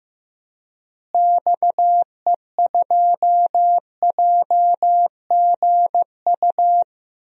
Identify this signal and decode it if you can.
Morse code, heard as a keyed tone, about 15 wpm, 700 Hz. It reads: XE2JGU
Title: DXCC-100 Call Signs - 6 Characters - Repeated 2x - 15wpm